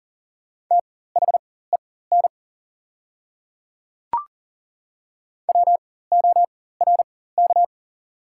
Morse code, heard as a keyed tone, about 40 wpm, spaced as for 19 wpm, 690 Hz. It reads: THEN WORK
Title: 4x Character Spacing - Top 200 Words - 40wpm